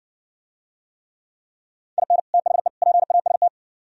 Morse code, heard as a keyed tone, about 55 wpm, 700 Hz. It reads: IN THE PAST